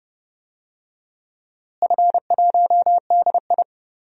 Morse code, heard as a keyed tone, about 32 wpm, 700 Hz. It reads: F1DI
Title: DXCC-100 Call Signs - 4 Characters - Repeated 3x - 30wpm